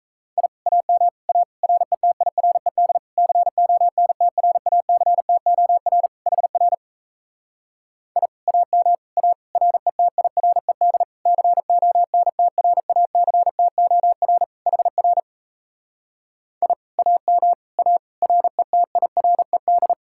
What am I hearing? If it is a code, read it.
Morse code, heard as a keyed tone, about 38 wpm, 700 Hz. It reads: I AM A RETIRED CONTRACTOR HR I AM A RETIRED CONTRACTOR HR I AM A RETIRED